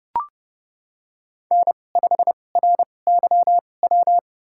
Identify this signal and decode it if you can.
Morse code, heard as a keyed tone, about 30 wpm, spaced as for 22 wpm, 700 Hz. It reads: N5RYW
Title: USA Call Signs - 2x Character Spacing - 30wpm